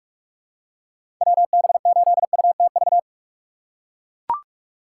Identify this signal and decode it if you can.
Morse code, heard as a keyed tone, about 43 wpm, 700 Hz. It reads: WB8UTV
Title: USA Call Signs - 6 Characters - 6x Character Spacing - 45wpm